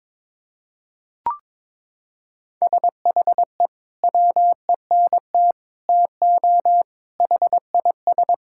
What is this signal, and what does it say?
Morse code, heard as a keyed tone, about 22 wpm, 700 Hz. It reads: SHE WENT TO HIS
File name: practice / 22wpm / Sentences with 2-Word Phrases - 22wpm